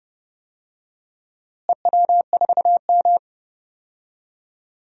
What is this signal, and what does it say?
Morse code, heard as a keyed tone, about 30 wpm, 700 Hz. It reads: EW4M
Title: DXCC-100 Call Signs - 4 Characters - Repeated 3x - 30wpm